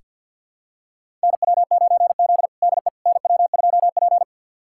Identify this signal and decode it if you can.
Morse code, heard as a keyed tone, about 49 wpm, 700 Hz. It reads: NW9Z DE NW2P
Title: USA Callsign DE Callsign - 4 Characters - 50wpm